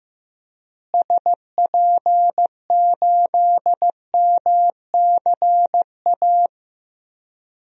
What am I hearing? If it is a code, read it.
Morse code, heard as a keyed tone, about 15 wpm, 700 Hz. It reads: SP8MCA
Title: DXCC-100 Call Signs - 6 Characters - Repeated 3x - 15wpm